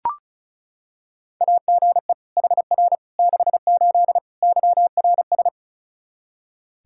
Morse code, heard as a keyed tone, about 36 wpm, 700 Hz. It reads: AGE HR 68 YRS
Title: Practice QSO - Exchange 3 - More Info - 35wpm